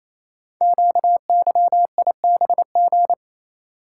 Morse code, heard as a keyed tone, about 28 wpm, 700 Hz. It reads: QYIBG
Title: Code Groups - 5 Letters - 28wpm